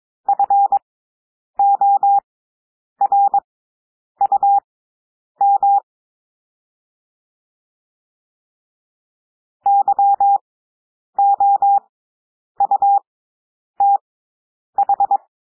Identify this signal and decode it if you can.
Morse code, heard as a keyed tone, about 22 wpm, spaced as for 9 wpm, 800 Hz. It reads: FORUM YOUTH
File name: cwmp3 / W2 SOUND WORDS 22 wpm